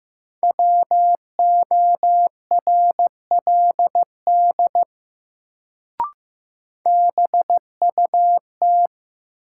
Morse code, heard as a keyed tone, about 15 wpm, 700 Hz. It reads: WORLD BUT
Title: Sets of 2 Words - New Word Drill - 15wpm